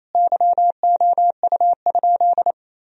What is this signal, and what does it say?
Morse code, heard as a keyed tone, about 28 wpm, 700 Hz. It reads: YOU?
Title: Sentences from Top 1000 Words - Review Part 2 - 28wpm